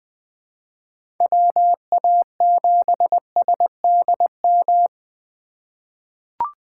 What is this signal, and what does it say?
Morse code, heard as a keyed tone, about 20 wpm, 700 Hz. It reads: WA7SDM
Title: USA Call Signs - 4x Character Spacing - 20wpm